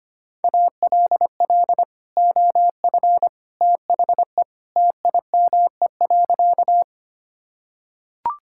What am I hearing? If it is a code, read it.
Morse code, heard as a keyed tone, about 25 wpm, 700 Hz. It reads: ALL OF THE TIME.
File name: practice / 25wpm / Sentences with 3-Word Phrases - 25wpm